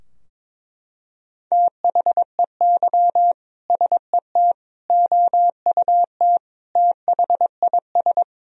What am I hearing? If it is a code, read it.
Morse code, heard as a keyed tone, about 23 wpm, 700 Hz. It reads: THEY SET OUT THIS
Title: Sentences from Top 1000 Words - Review Part 2 - 22wpm